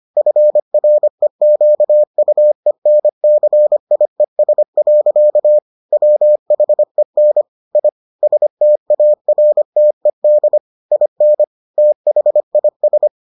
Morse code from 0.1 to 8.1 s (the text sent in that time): FREQUENCIES. WHEN I